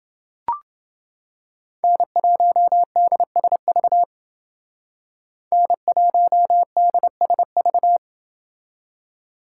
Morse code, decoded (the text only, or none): N1DSV N1DSV